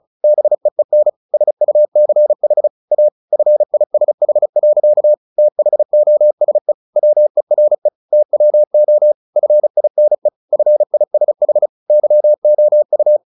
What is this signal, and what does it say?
Morse code, heard as a keyed tone, about 35 wpm, 600 Hz. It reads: BEEN SUCH A FISH. THOSE WERE TWO FINE FISH YOU
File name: the old man and the sea 2 35wpm